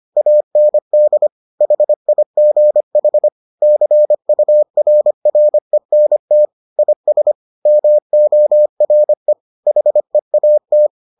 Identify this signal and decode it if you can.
Morse code, heard as a keyed tone, about 25 wpm, 600 Hz. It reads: AND HIGH CURRENT IS MORE HEAT